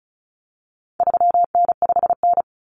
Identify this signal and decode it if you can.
Morse code, heard as a keyed tone, about 34 wpm, 700 Hz. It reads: 3N5N
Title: Common Cut Numbers - 4-Digit 35wpm